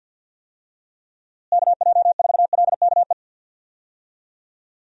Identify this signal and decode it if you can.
Morse code, heard as a keyed tone, about 50 wpm, 700 Hz. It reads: KJ4LKE
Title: USA Call Signs - 6 Characters - Repeated 2x - 50wpm